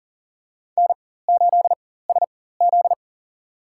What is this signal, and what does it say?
Morse code, heard as a keyed tone, about 40 wpm, spaced as for 19 wpm, 700 Hz. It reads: N8SZ
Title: USA Call Signs - 4 Characters - 4x Character Spacing - 40wpm